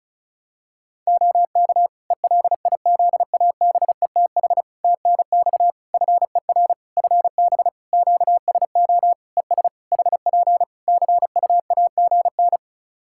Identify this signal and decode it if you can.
Morse code, heard as a keyed tone, about 36 wpm, 700 Hz. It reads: OK ELIZABETH TNX FER FB QSO ES HP CUAGN
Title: Practice QSO - Exchange 4 - Ending - 35wpm